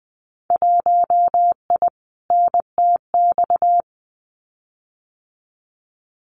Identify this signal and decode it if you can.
Morse code, heard as a keyed tone, about 20 wpm, 700 Hz. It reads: 1I NTX
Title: Winter Field Day Exchanges Encore 20wpm